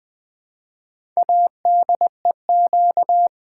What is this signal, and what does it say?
Morse code, heard as a keyed tone, about 20 wpm, 700 Hz. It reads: ADEQ
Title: Build A Word - 4 Letters At A Time - 10-Letter Words - 20wpm